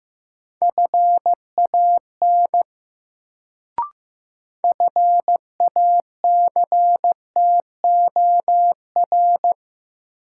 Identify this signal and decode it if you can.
Morse code, heard as a keyed tone, about 15 wpm, 700 Hz. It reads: FAN FACTOR
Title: Sentences from Top 1300 Words - New Word Drill Alt - 15wpm